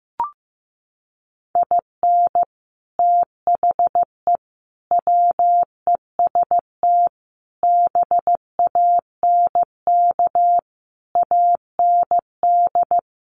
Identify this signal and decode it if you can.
Morse code, heard as a keyed tone, about 15 wpm, 700 Hz. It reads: IN THE WEST BANK AND
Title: Sets of 5 Words Encore2 15wpm